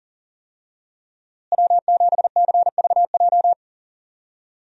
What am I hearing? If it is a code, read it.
Morse code, heard as a keyed tone, about 40 wpm, 700 Hz. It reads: W7CVJ
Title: USA Call Signs - 5 Characters - Repeated 2x - 40wpm